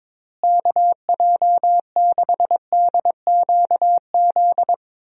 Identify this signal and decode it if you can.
Morse code, heard as a keyed tone, about 22 wpm, 700 Hz. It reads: KJ6DQZ
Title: USA Call Signs - 6 Characters - Repeated 2x - 22wpm